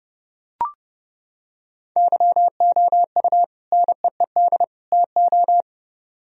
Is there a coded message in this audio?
Yes: Morse code from 2.0 to 5.6 s, heard as a keyed tone, about 28 wpm, 700 Hz. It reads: YOU NEED TO